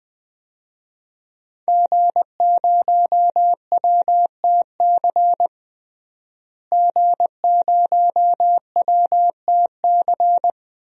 Morse code, heard as a keyed tone, about 20 wpm, 700 Hz. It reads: G0WTC G0WTC
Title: DXCC-100 Call Signs - 5 Characters - Repeated 2x - 20wpm